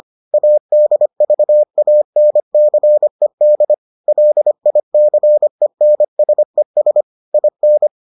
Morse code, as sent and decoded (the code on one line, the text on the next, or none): .- -.. ...- .- -. -.-. -.. / .-.. .. -.-. -. ... ... / .. -.
ADVANCED LICENSES IN